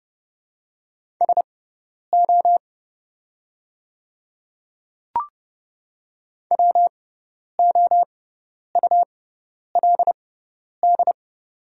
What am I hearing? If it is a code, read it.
Morse code, heard as a keyed tone, about 30 wpm, spaced as for 10 wpm, 700 Hz. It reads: SO WOULD
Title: 6x Character Spacing - Top 100 Words - 30wpm